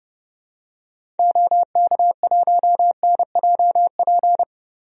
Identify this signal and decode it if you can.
Morse code, heard as a keyed tone, about 30 wpm, 700 Hz. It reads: OK1NJP
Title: DXCC-100 Call Signs - 6 Characters - Repeated 3x - 30wpm